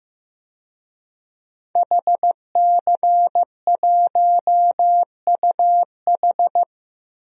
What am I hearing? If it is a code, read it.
Morse code, heard as a keyed tone, about 15 wpm, 700 Hz. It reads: HC1UH
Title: DXCC-100 Call Signs - 5 Characters - Repeated 3x - 15wpm